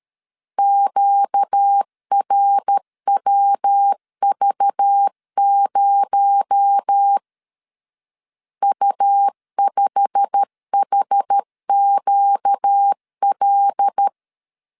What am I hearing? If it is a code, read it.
Morse code, heard as a keyed tone, about 13 wpm, 800 Hz. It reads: QRWV0 U5HQL
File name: CW / DET / 1 9 1500C S A 12wpm 3800